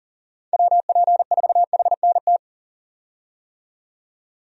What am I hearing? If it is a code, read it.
Morse code, heard as a keyed tone, about 38 wpm, 700 Hz. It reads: WP4HNT
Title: USA Call Signs - 6 Characters - 40wpm